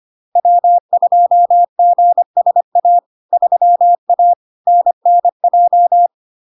Morse code, heard as a keyed tone, about 25 wpm, 700 Hz. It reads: W2GSA 3A NNJ
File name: Log-in-CW-25wpm-700-Hz